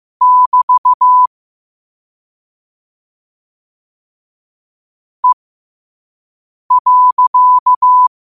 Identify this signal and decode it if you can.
Morse code, heard as a keyed tone, about 15 wpm, spaced as for 5 wpm, 1000 Hz. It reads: <BT> E.
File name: qotdfiles / 5wpm